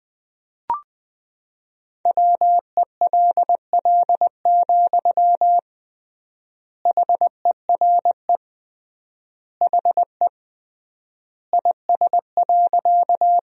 Morse code, heard as a keyed tone, about 21 wpm, 700 Hz. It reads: WELL, HERE HE IS.
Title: 3x Word-Spacing - Sentences from Top 200 Words - 20wpm